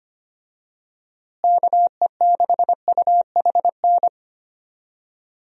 Morse code, heard as a keyed tone, about 25 wpm, 700 Hz. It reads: KE6UHN
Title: USA Call Signs - 6 Characters - Repeated 3x - 25wpm